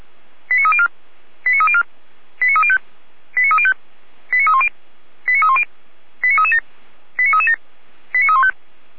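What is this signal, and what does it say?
Five-tone selective call (ZVEI), heard as quick runs of five tones, tones of about 70 ms. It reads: 89304 89304 89305 89305 89310 89310 89337 89337 89315
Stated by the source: ZVEI five-tone selective call received by SDR, from the Feuerwehr Orlamünde fire brigade audio archive.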